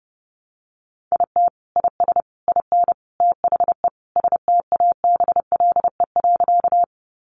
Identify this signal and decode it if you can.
Morse code, heard as a keyed tone, about 30 wpm, 700 Hz. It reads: IT IS IN THE STABLE.
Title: Sentences from Top 1500 Words 30wpm